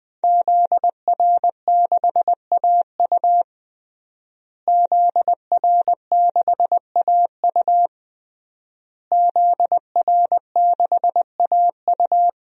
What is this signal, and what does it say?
Morse code, heard as a keyed tone, about 21 wpm, 700 Hz. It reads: ZR6AU ZR6AU ZR6AU
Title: DXCC-100 Call Signs - 5 Characters - Repeated 3x - 20wpm